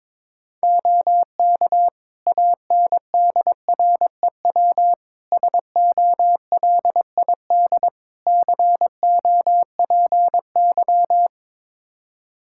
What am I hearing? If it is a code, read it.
Morse code, heard as a keyed tone, about 22 wpm, 700 Hz. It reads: OK ANDREW SOLID COPY
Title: Practice QSO - Exchange 3 - More Info - 22wpm